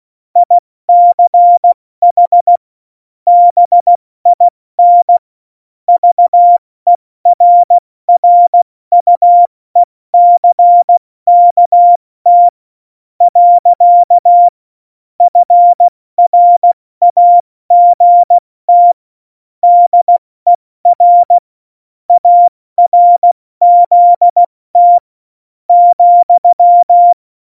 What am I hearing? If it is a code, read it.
Morse code, heard as a keyed tone, about 16 wpm, 700 Hz. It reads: ICH BIN VERRUECKT . FRAGT DER ARZT ,
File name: Morsekurs / Witze / Witz 06 March 2018, 70 F80